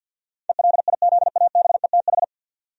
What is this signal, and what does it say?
Morse code, heard as a keyed tone, about 52 wpm, 700 Hz. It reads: ELIZABETH
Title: USA Common Names - 50wpm